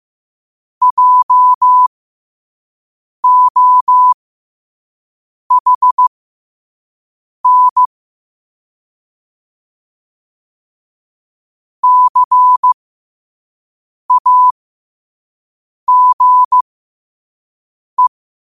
Morse code, heard as a keyed tone, about 15 wpm, spaced as for 5 wpm, 1000 Hz. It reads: JOHN CAGE
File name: qotdfiles / 5wpm